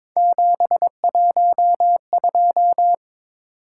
Morse code, heard as a keyed tone, about 22 wpm, 700 Hz. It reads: 712
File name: practice / 22wpm / Three Letter-Number Combinations - 22wpm